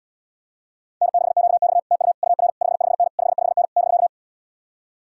Morse code, heard as a keyed tone, about 77 wpm, 700 Hz. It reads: N6QX TU UR 55N 55N <BK>